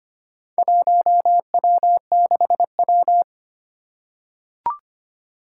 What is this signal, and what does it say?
Morse code, heard as a keyed tone, about 25 wpm, 700 Hz. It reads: 1W6W
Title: Four Challenging Letter-Number Combinations 25wpm